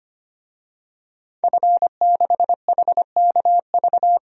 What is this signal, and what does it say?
Morse code, heard as a keyed tone, about 25 wpm, 700 Hz. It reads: F6HKV
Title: DXCC-100 Call Signs - 5 Characters - 25wpm